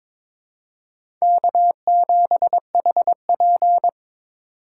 Morse code, heard as a keyed tone, about 22 wpm, 700 Hz. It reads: K7HP
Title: USA Call Signs - 4 Characters - 3x Character Spacing - 22wpm